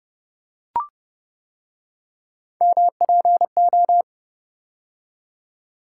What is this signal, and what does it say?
Morse code, heard as a keyed tone, about 29 wpm, 700 Hz. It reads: MPO